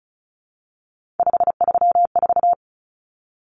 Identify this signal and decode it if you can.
Morse code, heard as a keyed tone, about 34 wpm, 700 Hz. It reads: 534
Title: Numbers - 3-Digit 35wpm